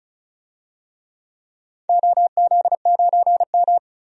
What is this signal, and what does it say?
Morse code, heard as a keyed tone, about 36 wpm, 700 Hz. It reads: OZ9M
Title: DXCC-100 Call Signs - 4 Characters - 35wpm